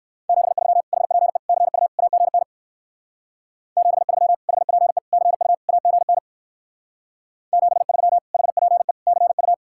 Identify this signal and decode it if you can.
Morse code, heard as a keyed tone, about 65 wpm, 700 Hz. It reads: 73 HPE CU AGN 73 HPE CU AGN 73 HPE CU